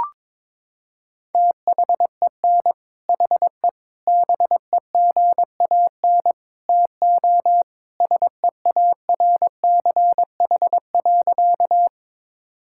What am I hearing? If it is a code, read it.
Morse code, heard as a keyed tone, about 22 wpm, 700 Hz. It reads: THEN HE BEGAN TO SEARCH.
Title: Sentences from Top 1000 Words - Review Part 5 - 22wpm